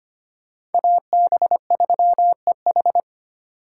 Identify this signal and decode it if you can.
Morse code, heard as a keyed tone, about 29 wpm, 700 Hz. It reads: AB3EH
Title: Speed Racing - USA 2x2 Call Signs 25wpm